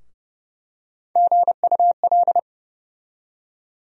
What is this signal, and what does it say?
Morse code, heard as a keyed tone, about 32 wpm, 700 Hz. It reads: GUL
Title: USA Call Sign Postfixes - 3 Letters 30wpm